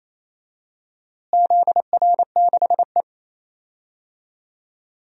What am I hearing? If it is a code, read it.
Morse code, heard as a keyed tone, about 28 wpm, 700 Hz. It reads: ZR6E
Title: DXCC-100 Call Signs - 4 Characters - Repeated 3x - 28wpm